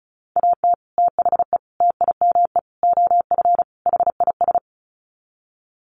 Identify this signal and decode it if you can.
Morse code, heard as a keyed tone, about 34 wpm, 700 Hz. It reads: AT THE TIME OF HIS
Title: Sets of 5 Words Encore2 35wpm